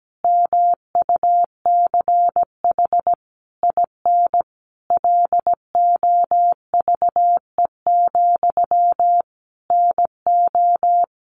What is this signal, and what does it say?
Morse code, heard as a keyed tone, about 17 wpm, 700 Hz. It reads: MUCH IN LOVE, NO
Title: Sentences from Top 1500 Words 17wpm